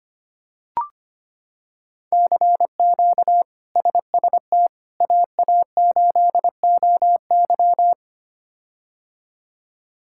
Morse code, heard as a keyed tone, about 25 wpm, 700 Hz. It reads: CQ SST AA8OY
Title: K1USN SST QSO Practice - 25wpm